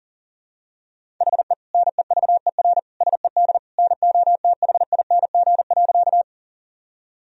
Morse code, heard as a keyed tone, about 40 wpm, 700 Hz. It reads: HE NEVER SED NOTHING.
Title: Sentences from Top 500 Words and Abbreviations 40wpm